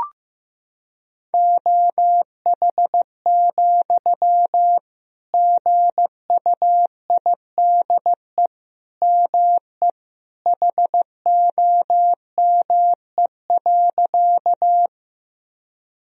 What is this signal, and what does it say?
Morse code, heard as a keyed tone, about 15 wpm, 700 Hz. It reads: OH, GUIDE ME HOME.
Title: Sentences from Top 1000 Words - Review Part 3 - 15wpm